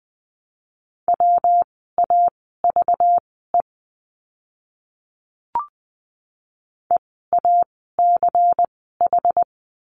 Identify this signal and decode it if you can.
Morse code, heard as a keyed tone, about 20 wpm, spaced as for 14 wpm, 700 Hz. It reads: WAVE EACH
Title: 192 Lesson 31f - Letter v - Words 20wpm